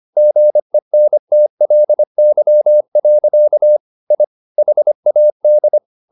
Morse code, heard as a keyed tone, about 25 wpm, 600 Hz. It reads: GENTLY. I HAD
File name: the old man and the sea part 1 25wpm